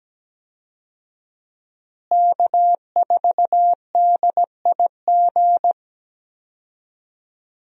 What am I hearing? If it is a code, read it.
Morse code, heard as a keyed tone, about 17 wpm, 700 Hz. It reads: K4DIG
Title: USA Call Signs - 5 Characters - Repeated 2x - 17wpm